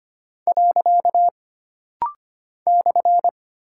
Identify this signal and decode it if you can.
Morse code, heard as a keyed tone, about 25 wpm, 700 Hz. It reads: . /
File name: Single Punctuation-Prosign - Rapid-Fire - 25wpm